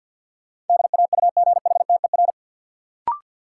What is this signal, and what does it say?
Morse code, heard as a keyed tone, about 51 wpm, 700 Hz. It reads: DAUGHTER
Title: Sentences from Top 1500 Words - New Word Drill Alt - 50wpm